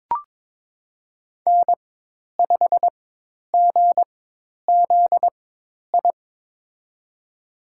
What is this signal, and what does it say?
Morse code, heard as a keyed tone, about 22 wpm, spaced as for 10 wpm, 700 Hz. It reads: N5GZI
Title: USA Call Signs - 4x Character Spacing - 22wpm